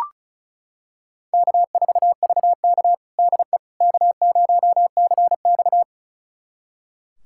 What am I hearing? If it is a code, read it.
Morse code, heard as a keyed tone, about 36 wpm, 700 Hz. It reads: K4VK DE K0CX
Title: USA Callsign DE Callsign - 4 Characters - 35wpm